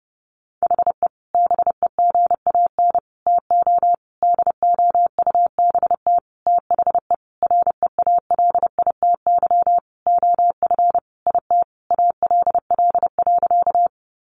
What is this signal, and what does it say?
Morse code, heard as a keyed tone, about 30 wpm, 700 Hz. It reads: HE BEGAN TO DOUBT THE REALITY OF IT ALL.
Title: Sentences from Top 1500 Words 30wpm